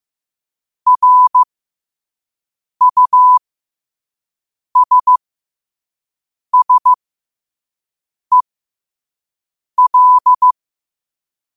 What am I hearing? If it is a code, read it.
Morse code, heard as a keyed tone, about 15 wpm, spaced as for 5 wpm, 1000 Hz. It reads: RUSSEL
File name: qotdfiles / 5wpm